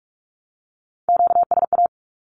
Morse code, heard as a keyed tone, about 44 wpm, 700 Hz. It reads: QSA
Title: Q-Codes 45wpm